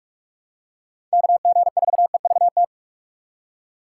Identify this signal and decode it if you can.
Morse code, heard as a keyed tone, about 47 wpm, 700 Hz. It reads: KG4EVT
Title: USA Call Signs - 6x Character Spacing - 45wpm